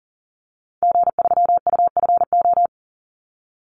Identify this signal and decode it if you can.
Morse code, heard as a keyed tone, about 40 wpm, 700 Hz. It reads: G3UFO
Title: Call Signs - 40wpm